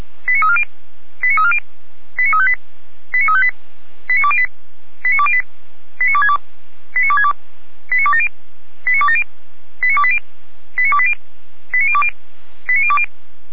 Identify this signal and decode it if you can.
Five-tone selective call (ZVEI), heard as quick runs of five tones, tones of about 80 ms. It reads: 89340 89340 89358 89358 89208 89208 89262 89262 89270 89270 89280 89280 89020 89020